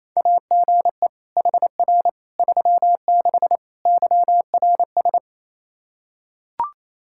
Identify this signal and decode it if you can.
Morse code, heard as a keyed tone, about 28 wpm, 700 Hz. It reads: AGE HR 36 YRS
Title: Practice QSO - Exchange 3 - More Info - 28wpm